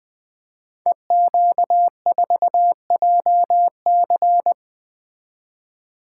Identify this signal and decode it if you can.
Morse code, heard as a keyed tone, about 20 wpm, 700 Hz. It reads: EQ4JC
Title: Code Groups - 5 Letter:Numbers - 20wpm